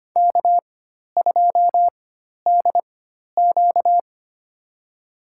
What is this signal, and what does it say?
Morse code, heard as a keyed tone, about 25 wpm, spaced as for 12 wpm, 700 Hz. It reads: K2DQ
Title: USA Call Signs - 4 Characters - 4x Character Spacing - 25wpm